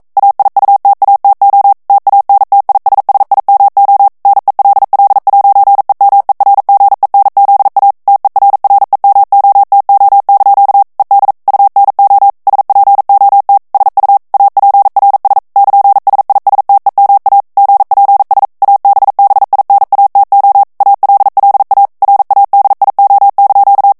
Morse code, heard as a keyed tone, about 42 wpm, 800 Hz. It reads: AIUTATO TANTISSIMO NELL'EMERGENZA TERREMOTO) ED UNO SPOT SU APRS <KN>SISTEMA GPS ABBINATO ALLA RADIO)
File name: morse mp3 / ITALIEN texte clair 1 vitesse 40 mots minute